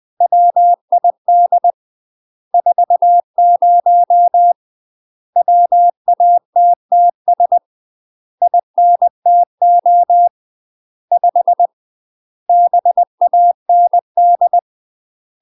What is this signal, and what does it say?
Morse code, heard as a keyed tone, about 20 wpm, 700 Hz. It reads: WID 40 WATTS INTO 5 BAND
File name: UK MASTER QSOs TRAINING / 20WPM / UK MASTER QSOs 0008